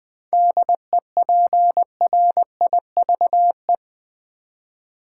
Morse code, heard as a keyed tone, about 20 wpm, 690 Hz. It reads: DEPRIVE
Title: 7 Letter Words - First 3 Letters Given - 20wpm